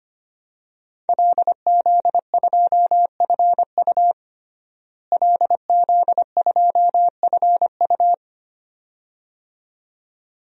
Morse code, heard as a keyed tone, about 25 wpm, 700 Hz. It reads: LZ2FU LZ2FU
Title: DXCC-100 Call Signs - 5 Characters - Repeated 2x - 25wpm